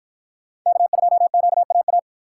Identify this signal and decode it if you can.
Morse code, heard as a keyed tone, about 49 wpm, 700 Hz. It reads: K2QAA